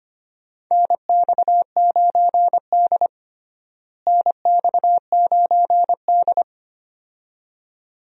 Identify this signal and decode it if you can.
Morse code, heard as a keyed tone, about 25 wpm, 700 Hz. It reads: NX9D NX9D
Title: USA Call Signs - 4 Characters - Repeated 2x - 25wpm